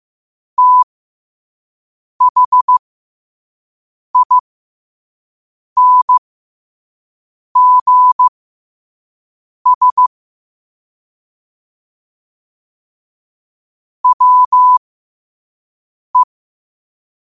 Morse code, heard as a keyed tone, about 15 wpm, spaced as for 5 wpm, 1000 Hz. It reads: THINGS WE
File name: qotdfiles / 5wpm